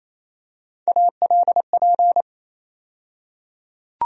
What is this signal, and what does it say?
Morse code, heard as a keyed tone, about 27 wpm, 700 Hz. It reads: ALP
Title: USA Call Sign Postfixes - 3 Letters 28wpm